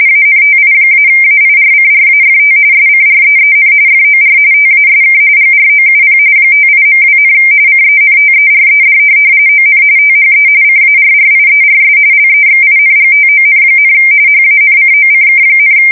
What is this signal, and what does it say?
RTTY, about 45.45 baud. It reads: ANTI-AIRCRAFT SCREEN FOR TASK GROUP 58.2, CENTERED ON THE CARRIERS LEXINGTON, HANCOCK,
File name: fsk170 / USS-Missouri